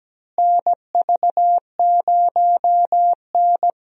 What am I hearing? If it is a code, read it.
Morse code, heard as a keyed tone, about 17 wpm, 700 Hz. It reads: NV0N
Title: USA Call Signs - 4 Characters - Repeated 3x - 17wpm